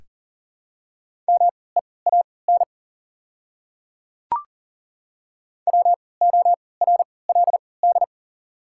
Morse code, heard as a keyed tone, about 40 wpm, spaced as for 23 wpm, 690 Hz. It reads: MEAN WORLD
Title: 3x Character Spacing - Top 200 Words - 40wpm